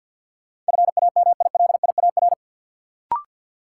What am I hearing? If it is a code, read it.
Morse code, heard as a keyed tone, about 52 wpm, 700 Hz. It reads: FAMILIAR
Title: Sentences from Top 1000 Words - New Word Drill - 50wpm